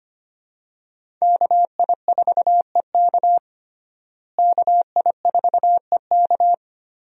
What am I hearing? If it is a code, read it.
Morse code, heard as a keyed tone, about 25 wpm, 700 Hz. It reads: KI4EK KI4EK
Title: USA Call Signs - 5 Characters - Repeated 3x - 25wpm